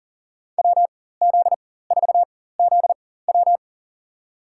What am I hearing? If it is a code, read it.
Morse code, heard as a keyed tone, about 43 wpm, spaced as for 19 wpm, 700 Hz. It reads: WZ4ZW
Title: USA Call Signs - 4x Character Spacing - 40wpm